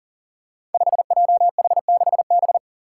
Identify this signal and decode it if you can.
Morse code, heard as a keyed tone, about 42 wpm, 700 Hz. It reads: 5JH6B
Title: Five Challenging Letter-Number Combinations 40wpm